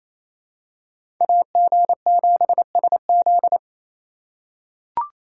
Morse code, heard as a keyed tone, about 28 wpm, 700 Hz. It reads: AG7SZ
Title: USA Call Signs - 6x Character Spacing - 28wpm